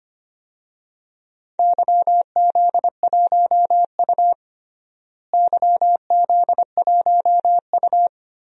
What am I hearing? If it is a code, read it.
Morse code, heard as a keyed tone, about 25 wpm, 700 Hz. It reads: YZ1U YZ1U
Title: DXCC-100 Call Signs - 4 Characters - Repeated 3x - 25wpm